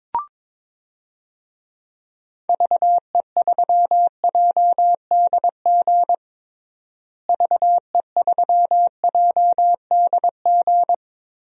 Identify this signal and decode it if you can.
Morse code, heard as a keyed tone, about 22 wpm, 700 Hz. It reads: VE3JDG VE3JDG
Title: DXCC-100 Call Signs - 6 Characters - Repeated 3x - 22wpm